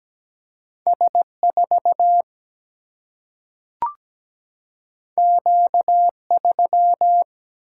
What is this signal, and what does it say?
Morse code, heard as a keyed tone, about 17 wpm, 700 Hz. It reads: S 4 Q 3
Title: Single-Letter Single-Number 17wpm